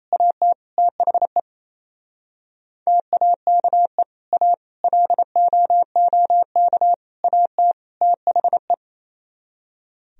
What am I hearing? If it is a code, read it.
Morse code, heard as a keyed tone, about 28 wpm, 700 Hz. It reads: AT THE TAKE A LOOK AT THE
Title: Speed-Racing - Sets of 5 Words 28wpm